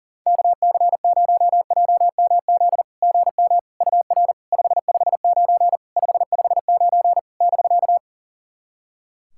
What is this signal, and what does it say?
Morse code, heard as a keyed tone, about 40 wpm, 690 Hz. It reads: KC0JMZ GM UR 559 559 <BK>
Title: SOTA QSO Practice 40wpm